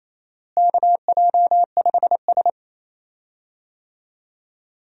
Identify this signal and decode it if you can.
Morse code, heard as a keyed tone, about 28 wpm, 700 Hz. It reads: KJ5S
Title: USA Call Signs - 4 Characters - Repeated 2x - 28wpm